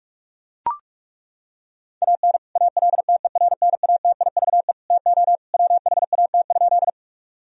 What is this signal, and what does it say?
Morse code, heard as a keyed tone, about 41 wpm, 700 Hz. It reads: AN ALTERNATIVE TO WHAT?
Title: Sentences from Top 1000 Words - Review Part 4 - 45wpm